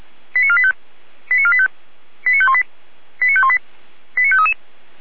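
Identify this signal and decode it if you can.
Five-tone selective call (ZVEI), heard as quick runs of five tones, tones of about 70 ms. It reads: 89485 89485 89518 89518 89533
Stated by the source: ZVEI five-tone call 89020 received on SDR.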